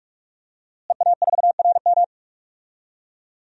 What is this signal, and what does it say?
Morse code, heard as a keyed tone, about 46 wpm, 700 Hz. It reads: EA4RM
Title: DXCC-100 Call Signs - 5 Characters - Repeated 3x - 45wpm